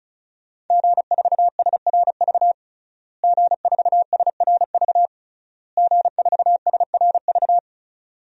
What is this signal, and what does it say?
Morse code, heard as a keyed tone, about 36 wpm, 700 Hz. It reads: G4SRV G4SRV G4SRV
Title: DXCC-100 Call Signs - 5 Characters - Repeated 3x - 35wpm